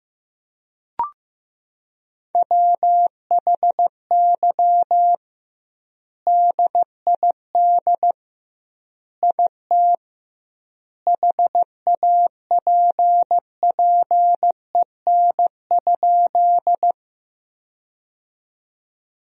Morse code, heard as a keyed tone, about 15 wpm, 700 Hz. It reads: WHY DID IT HAPPEN?
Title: 2x Word-Spacing - Sentences from Top 400 Words - 15wpm